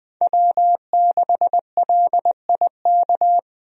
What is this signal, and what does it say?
Morse code, heard as a keyed tone, about 20 wpm, 700 Hz. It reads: W6LIK
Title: USA Call Signs - 5 Characters - 20wpm